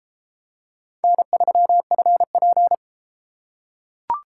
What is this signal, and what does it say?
Morse code, heard as a keyed tone, about 31 wpm, 700 Hz. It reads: N3FP